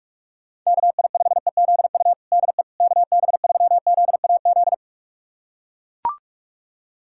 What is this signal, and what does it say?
Morse code, heard as a keyed tone, about 44 wpm, 700 Hz. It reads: KI5EZU DE KB3ZAZ
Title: USA Callsign DE Callsign - 6 Characters - 45wpm